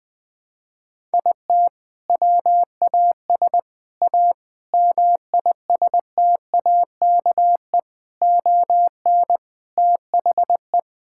Morse code, heard as a keyed tone, about 20 wpm, 700 Hz. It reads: IT WAS A MISTAKE ON THE
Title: Sentences with 4-Word Phrases - 20wpm